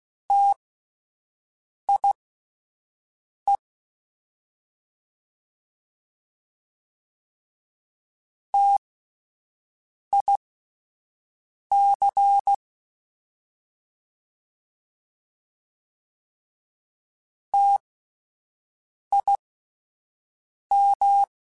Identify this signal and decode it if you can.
Morse code, heard as a keyed tone, about 16 wpm, spaced as for 5 wpm, 800 Hz. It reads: TIE TIC TIM